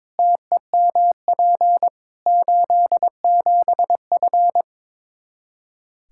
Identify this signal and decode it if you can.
Morse code, heard as a keyed tone, about 22 wpm, 700 Hz. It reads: TEMP 87F
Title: Practice QSO - Exchange 2 - Info - 22wpm